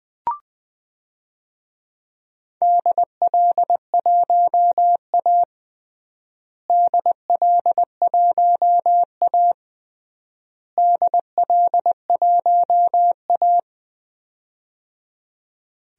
Morse code, heard as a keyed tone, about 20 wpm, 700 Hz. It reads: DL1A DL1A DL1A
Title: DXCC-100 Call Signs - 4 Characters - Repeated 3x - 20wpm